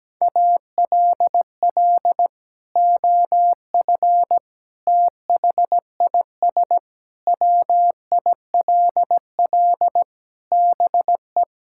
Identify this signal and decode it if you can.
Morse code, heard as a keyed tone, about 17 wpm, 700 Hz. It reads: ALL OF THIS WILL BE
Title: Sentences from Top 1000 Words - Review Part 6 - 17wpm